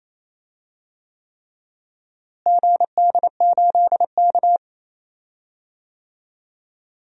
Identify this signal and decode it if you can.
Morse code, heard as a keyed tone, about 27 wpm, 700 Hz. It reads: GD8K
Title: DXCC-100 Call Signs - 4 Characters - Repeated 3x - 28wpm